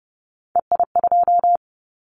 Morse code, heard as a keyed tone, about 30 wpm, 700 Hz. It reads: EI2
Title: Callsign Prefixes for Top 50 DXCC - 30wpm